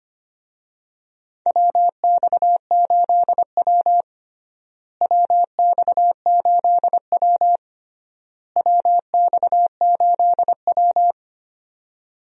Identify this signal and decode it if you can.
Morse code, heard as a keyed tone, about 25 wpm, 700 Hz. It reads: WX8W WX8W WX8W
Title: USA Call Signs - 4 Characters - Repeated 3x - 25wpm